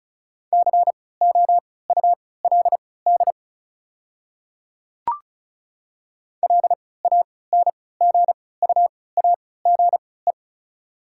Morse code, heard as a keyed tone, about 36 wpm, spaced as for 20 wpm, 690 Hz. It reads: COULD LANGUAGE
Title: 3x Character Spacing - Top 500 Words - 35wpm